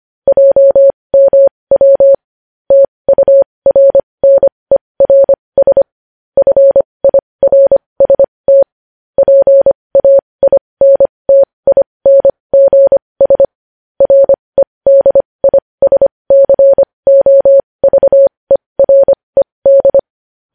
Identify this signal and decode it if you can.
Morse code, heard as a keyed tone, about 25 wpm, spaced as for 20 wpm, 550 Hz. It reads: JMW TURNERS FIRST PAINTINGS REDISCOVERED
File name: podcast / News headlines-1x-25x20